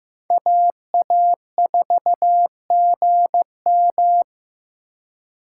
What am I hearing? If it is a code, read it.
Morse code, heard as a keyed tone, about 15 wpm, 700 Hz. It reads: AA4GM
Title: USA Call Signs - 5 Characters - 4x Character Spacing - 15wpm